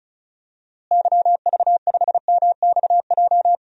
Morse code, heard as a keyed tone, about 36 wpm, 700 Hz. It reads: YV5MXJ
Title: DXCC-100 Call Signs - 6 Characters - Repeated 2x - 35wpm